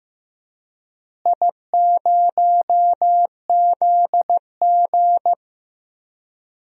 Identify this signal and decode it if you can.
Morse code, heard as a keyed tone, about 15 wpm, 700 Hz. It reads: I0ZG
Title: DXCC-100 Call Signs - 4 Characters - Repeated 3x - 15wpm